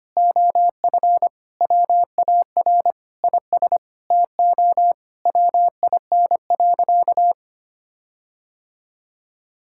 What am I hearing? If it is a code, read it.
Morse code, heard as a keyed tone, about 25 wpm, 700 Hz. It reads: OF WAR IS TO WIN.
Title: Sentences from Top 1600 Words 25wpm